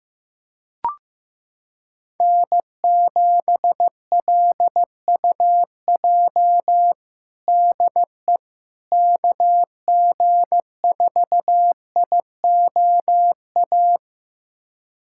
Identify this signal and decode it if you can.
Morse code, heard as a keyed tone, about 15 wpm, 700 Hz. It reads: N7LUJ DE KG4IOA
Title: USA Callsign DE Callsign 15wpm